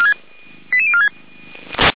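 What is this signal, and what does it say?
Five-tone selective call (ZVEI), heard as quick runs of five tones, tones of about 80 ms. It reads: 88046
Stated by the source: ZVEI five-tone call received by SDR at feuerwehr-orlamuende.org, fire brigade Orlamünde.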